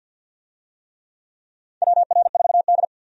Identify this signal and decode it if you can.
Morse code, heard as a keyed tone, about 52 wpm, 700 Hz. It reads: WR4D
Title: USA Call Signs - 4 Characters - 50wpm